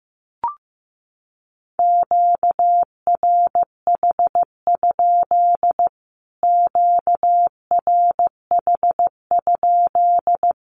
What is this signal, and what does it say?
Morse code, heard as a keyed tone, about 15 wpm, 700 Hz. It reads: QRH? QRH?
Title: Q-Codes Repeated 3x 15wpm